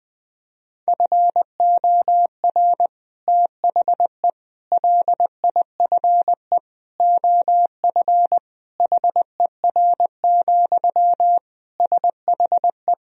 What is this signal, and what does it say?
Morse code, heard as a keyed tone, about 20 wpm, 700 Hz. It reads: FOR THE LIFE OF HER, SHE
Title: Sentences from Top 1000 Words - Review Part 2 - 20wpm